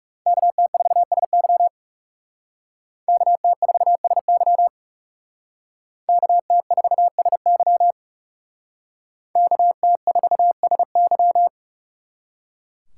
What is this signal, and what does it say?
Morse code, heard as a keyed tone, about 38 wpm, 700 Hz. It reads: KT4SY KT4SY KT4SY KT4SY